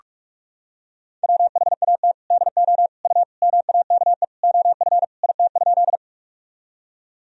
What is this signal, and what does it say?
Morse code, heard as a keyed tone, about 41 wpm, 700 Hz. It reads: WHAT DO U MAKE OF IT?
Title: Sentences from Top 100 Words and Abbreviations 45wpm